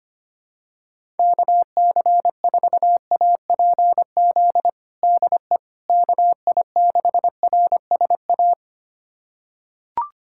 Morse code, heard as a keyed tone, about 25 wpm, 700 Hz. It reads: KC4APZ DE KI6RSA
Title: USA Callsign DE Callsign - 6 Characters - 25wpm